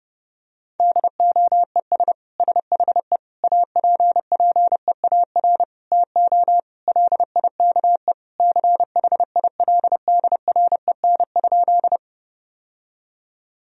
Morse code, heard as a keyed tone, about 30 wpm, 700 Hz. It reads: DOES SHE APPEAR TO LIKE CHILDREN?
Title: Sentences from Top 1000 Words - Review Part 2 - 30wpm